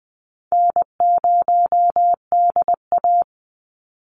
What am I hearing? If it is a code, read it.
Morse code, heard as a keyed tone, about 19 wpm, 700 Hz. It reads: N0DA
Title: NA SOTA Chaser Callsigns 20wpm